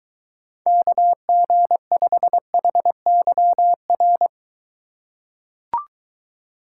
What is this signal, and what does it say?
Morse code, heard as a keyed tone, about 22 wpm, 700 Hz. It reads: KG5HYR